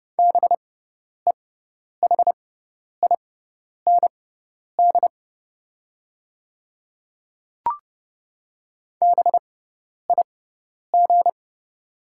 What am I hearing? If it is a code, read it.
Morse code, heard as a keyed tone, about 30 wpm, spaced as for 10 wpm, 700 Hz. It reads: BEHIND BIG
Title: 6x Character Spacing - Top 500 Words - 30wpm